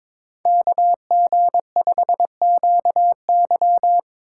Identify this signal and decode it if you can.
Morse code, heard as a keyed tone, about 22 wpm, 700 Hz. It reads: KG5QY
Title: USA Call Signs - 5 Characters - 8x Character Spacing - 22wpm